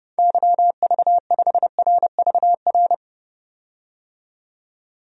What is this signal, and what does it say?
Morse code, heard as a keyed tone, about 29 wpm, 700 Hz. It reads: YV5RVR
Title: DXCC-100 Call Signs - 6 Characters - 30wpm